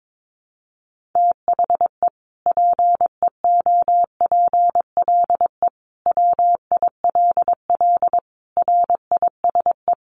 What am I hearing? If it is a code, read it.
Morse code, heard as a keyed tone, about 22 wpm, 700 Hz. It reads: THE PEOPLE WILL RISE
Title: Sentences from Top 900 Words 22wpm